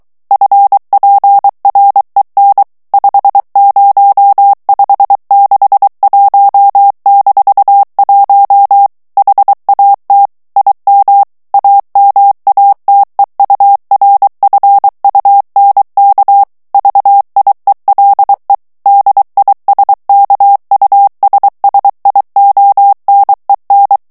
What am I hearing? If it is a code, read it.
Morse code, heard as a keyed tone, about 23 wpm, 800 Hz. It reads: FPREN 50561-1 HAT IM AMATEURFUNK VIELE DISKUSSIONEN
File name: ALLEMAND texte clair 1 vitesse 22 mots minute